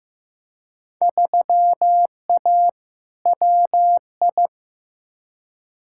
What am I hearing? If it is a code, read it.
Morse code, heard as a keyed tone, about 15 wpm, 700 Hz. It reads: 3A WI
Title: Field Day Exchanges - 15wpm